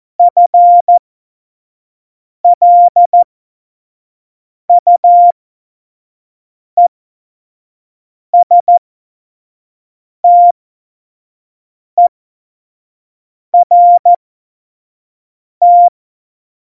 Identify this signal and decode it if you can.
Morse code, heard as a keyed tone, about 14 wpm, spaced as for 5 wpm, 700 Hz. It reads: FLUESTERT
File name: Morsekurs / Witze / Witz 23. Januar 2018, 25 F70